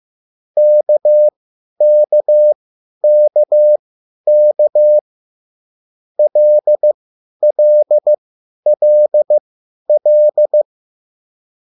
Morse code, heard as a keyed tone, about 15 wpm, spaced as for 10 wpm, 600 Hz. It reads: KKKK LLLL